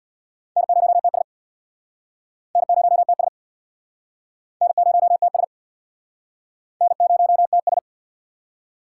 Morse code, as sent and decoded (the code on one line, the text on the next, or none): -. ----- - ... / -. ----- - ... / -. ----- - ... / -. ----- - ...
N0TS N0TS N0TS N0TS